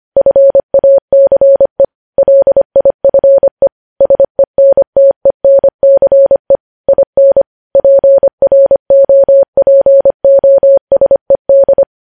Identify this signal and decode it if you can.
Morse code, heard as a keyed tone, about 25 wpm, 550 Hz. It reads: FACE LIFE SENTENCE IN PROPOSED
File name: podcast / News headlines-1x-25x25